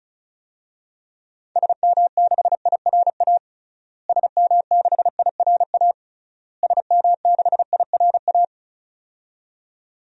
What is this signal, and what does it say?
Morse code, heard as a keyed tone, about 36 wpm, 700 Hz. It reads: SM6IRA SM6IRA SM6IRA
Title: DXCC-100 Call Signs - 6 Characters - Repeated 3x - 35wpm